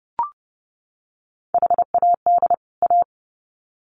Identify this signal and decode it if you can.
Morse code, heard as a keyed tone, about 30 wpm, 700 Hz. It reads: HAD A
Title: Sets of 2 Words 30wpm